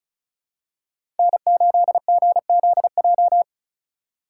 Morse code, heard as a keyed tone, about 36 wpm, 700 Hz. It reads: N8GZJ
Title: USA Call Signs - 5 Characters - 4x Character Spacing - 35wpm